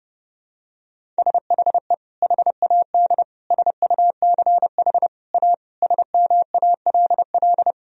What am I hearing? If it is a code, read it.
Morse code, heard as a keyed tone, about 30 wpm, 700 Hz. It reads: SHE HAD SUCH A SMALL
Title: Sentences from Top 1000 Words - Review Part 1 - 30wpm